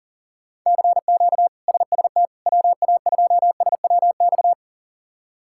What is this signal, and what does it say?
Morse code, heard as a keyed tone, about 40 wpm, 700 Hz. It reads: CQ SST WA2SWX
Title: K1USN SST QSO Practice - 40wpm